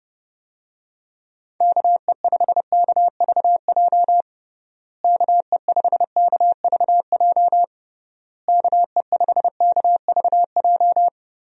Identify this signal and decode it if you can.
Morse code, heard as a keyed tone, about 30 wpm, 700 Hz. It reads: KE5KVJ KE5KVJ KE5KVJ
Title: USA Call Signs - 6 Characters - Repeated 3x - 30wpm